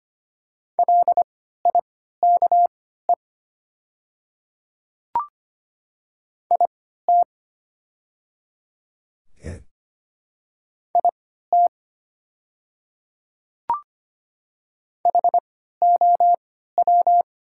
Morse code, heard as a keyed tone, about 25 wpm, spaced as for 14 wpm, 700 Hz. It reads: LIKE IT IT HOW
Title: 3x Character Spacing - Top 100 Words - 25wpm